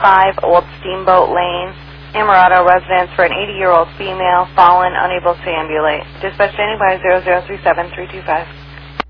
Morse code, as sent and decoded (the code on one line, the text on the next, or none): none